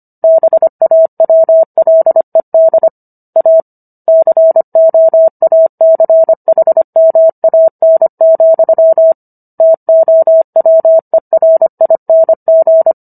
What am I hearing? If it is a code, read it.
Morse code, heard as a keyed tone, about 25 wpm, 650 Hz. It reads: BAWLED A COACHMAN, TOWERING